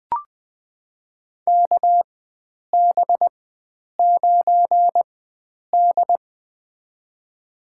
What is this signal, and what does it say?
Morse code, heard as a keyed tone, about 20 wpm, spaced as for 9 wpm, 700 Hz. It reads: KB9D
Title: USA Call Signs - 4x Character Spacing - 20wpm